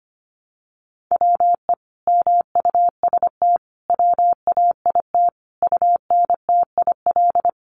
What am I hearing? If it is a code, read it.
Morse code, heard as a keyed tone, about 25 wpm, 700 Hz. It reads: WE MUST WAIT UNTIL
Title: Sentences from Top 1300 Words 25wpm